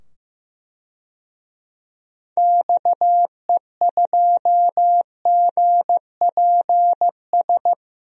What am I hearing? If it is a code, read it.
Morse code, heard as a keyed tone, about 15 wpm, 700 Hz. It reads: XE2GPS
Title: DXCC-100 Call Signs - 6 Characters - 15wpm